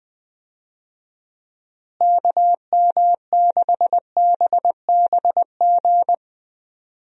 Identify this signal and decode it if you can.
Morse code, heard as a keyed tone, about 20 wpm, 700 Hz. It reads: KM6BBG
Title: USA Call Signs - 6 Characters - Repeated 2x - 20wpm